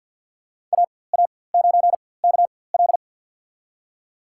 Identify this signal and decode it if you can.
Morse code, heard as a keyed tone, about 52 wpm, spaced as for 23 wpm, 700 Hz. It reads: AA9KL
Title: USA Call Signs - 5 Characters - 4x Character Spacing - 50wpm